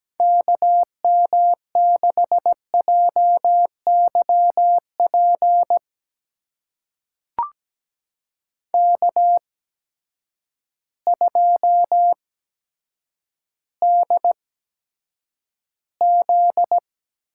Morse code, heard as a keyed tone, about 17 wpm, spaced as for 8 wpm, 700 Hz. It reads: KM6JYP K2DZ
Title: USA Call Signs - 8x Character Spacing - 17wpm